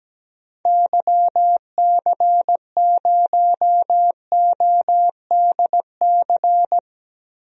Morse code, heard as a keyed tone, about 17 wpm, 700 Hz. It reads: YC0ODC
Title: DXCC-100 Call Signs - 6 Characters - 17wpm